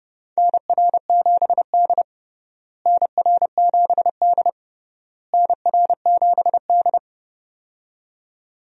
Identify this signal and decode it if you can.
Morse code, heard as a keyed tone, about 31 wpm, 700 Hz. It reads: NR7D NR7D NR7D
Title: USA Call Signs - 4 Characters - Repeated 3x - 30wpm